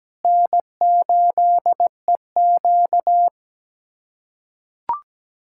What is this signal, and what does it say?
Morse code, heard as a keyed tone, about 17 wpm, 700 Hz. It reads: N8EQ
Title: USA Callsign DE Callsign - 4 Characters - 17wpm